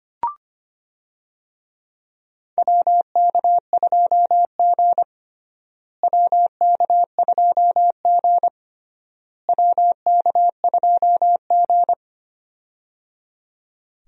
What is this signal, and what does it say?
Morse code, heard as a keyed tone, about 25 wpm, 700 Hz. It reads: WK2G WK2G WK2G
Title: USA Call Signs - 4 Characters - Repeated 3x - 25wpm